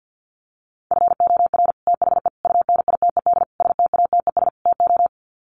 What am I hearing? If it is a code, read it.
Morse code, heard as a keyed tone, about 50 wpm, 700 Hz. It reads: FOR THE UNITED STATES TO
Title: Sets of 5 Words Encore 50wpm